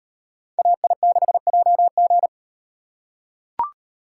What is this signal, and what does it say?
Morse code, heard as a keyed tone, about 37 wpm, 700 Hz. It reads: AI6JG